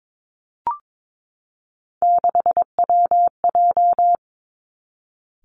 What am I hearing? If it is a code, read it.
Morse code, heard as a keyed tone, about 22 wpm, 700 Hz. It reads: 6WJ
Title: Three Challenging Letter-Number Combinations 22wpm